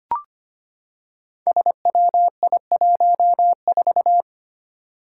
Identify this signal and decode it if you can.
Morse code, heard as a keyed tone, about 25 wpm, 700 Hz. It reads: SWI14
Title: Five Challenging Letter-Number Combinations 25wpm